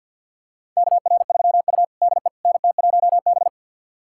Morse code, heard as a keyed tone, about 52 wpm, 700 Hz. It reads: KR3U DE NT1B
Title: USA Callsign DE Callsign - 4 Characters - 50wpm